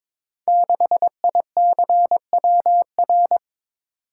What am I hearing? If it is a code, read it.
Morse code, heard as a keyed tone, about 22 wpm, 700 Hz. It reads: 6ICWR
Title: Code Groups - 5 Letter:Numbers - 22wpm